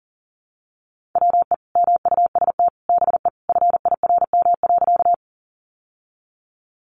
Morse code, heard as a keyed tone, about 40 wpm, 700 Hz. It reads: WE MUST BE FIRM.
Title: Sentences from Top 1400 Words 40wpm